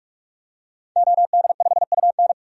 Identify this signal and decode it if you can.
Morse code, heard as a keyed tone, about 41 wpm, 700 Hz. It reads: OD5UN
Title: DXCC-100 Call Signs - 5 Characters - Repeated 3x - 45wpm